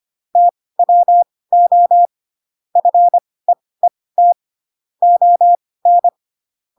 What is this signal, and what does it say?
Morse code, heard as a keyed tone, about 25 wpm, spaced as for 18 wpm, 700 Hz. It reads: TWO FEET ON
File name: Pinocchio / Pinocchio ch6 12 25 ik1hge